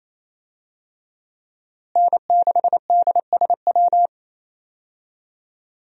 Morse code, heard as a keyed tone, about 27 wpm, 700 Hz. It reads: N6DSW
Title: USA Call Signs - 5 Characters - Repeated 3x - 28wpm